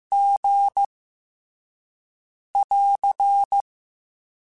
Morse code, heard as a keyed tone, about 15 wpm, 800 Hz. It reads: G <AR>
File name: Special INGs-mp3s / Special INGs-mp3s / VerySlowToFaster Ing @13